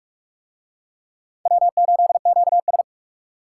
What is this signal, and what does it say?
Morse code, heard as a keyed tone, about 44 wpm, 700 Hz. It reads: W8QS